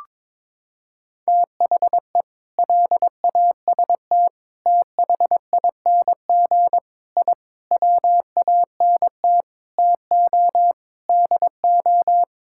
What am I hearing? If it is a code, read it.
Morse code, heard as a keyed tone, about 22 wpm, 700 Hz. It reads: THE LAST THING I WANT TO DO
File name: practice / 22wpm / Sets of 7 Words - 22wpm